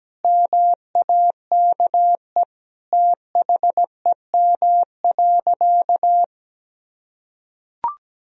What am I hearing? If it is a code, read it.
Morse code, heard as a keyed tone, about 17 wpm, 700 Hz. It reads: MAKE THEM.
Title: Sentences from Top 1000 Words - Review Part 1 - 17wpm